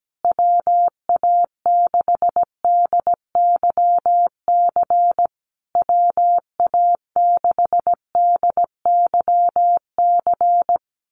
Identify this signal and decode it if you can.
Morse code, heard as a keyed tone, about 17 wpm, 700 Hz. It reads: WA6DYC WA6DYC
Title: USA Call Signs Repeated 3x 17wpm